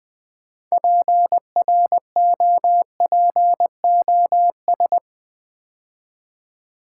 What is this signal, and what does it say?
Morse code, heard as a keyed tone, about 20 wpm, 700 Hz. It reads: PROPOS
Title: Build A Word - 3 Letters At A Time - 8-Letter Words - 20wpm